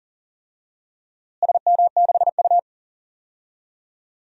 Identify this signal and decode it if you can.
Morse code, heard as a keyed tone, about 40 wpm, 700 Hz. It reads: SM6U
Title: DXCC-100 Call Signs - 4 Characters - Repeated 2x - 40wpm